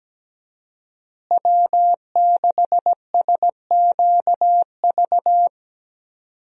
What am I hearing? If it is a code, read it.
Morse code, heard as a keyed tone, about 17 wpm, 700 Hz. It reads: W6SQV
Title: USA Call Signs - 5 Characters - Repeated 3x - 17wpm